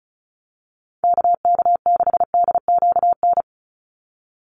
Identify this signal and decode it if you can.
Morse code, heard as a keyed tone, about 34 wpm, 700 Hz. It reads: KK6DQN
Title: USA Call Signs 35wpm